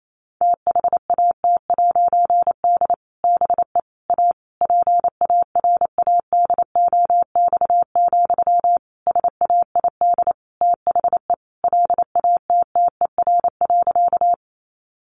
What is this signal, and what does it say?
Morse code, heard as a keyed tone, about 28 wpm, 700 Hz. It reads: THAT'D BE A PARADOX, SAID THE LATTER.
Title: A Sound Of Thunder 28wpm